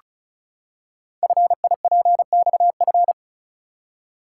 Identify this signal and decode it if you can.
Morse code, heard as a keyed tone, about 36 wpm, 700 Hz. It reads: FIPXF
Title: Code Groups - 5 Letters - 35wpm